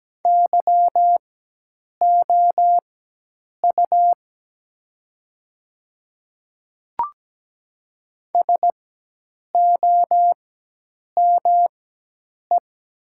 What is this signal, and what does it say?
Morse code, heard as a keyed tone, about 17 wpm, spaced as for 8 wpm, 700 Hz. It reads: YOU SOME
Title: 4x Character Spacing - Top 100 Words - 17wpm